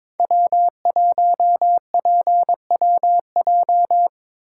Morse code, heard as a keyed tone, about 21 wpm, 700 Hz. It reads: W1PWJ
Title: USA Call Signs - 5 Characters - 8x Character Spacing - 22wpm